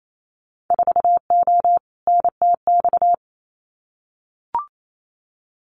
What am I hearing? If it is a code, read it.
Morse code, heard as a keyed tone, about 28 wpm, 700 Hz. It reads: 4O NTX
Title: Winter Field Day Exchanges 28wpm